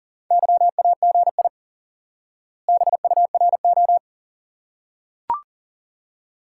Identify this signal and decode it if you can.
Morse code, heard as a keyed tone, about 40 wpm, 700 Hz. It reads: YAGI BURO
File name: IQE Course 3 - Lesson 89 - YAGI - Two Elements - 40wpm